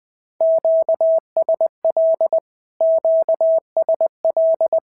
Morse code, heard as a keyed tone, about 20 wpm, 650 Hz. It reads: QSL QSL
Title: SOTA QSO Practice 20wpm